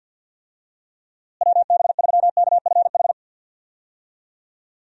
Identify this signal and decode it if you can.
Morse code, heard as a keyed tone, about 46 wpm, 700 Hz. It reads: WB3KFH
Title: USA Call Signs - 6 Characters - Repeated 2x - 50wpm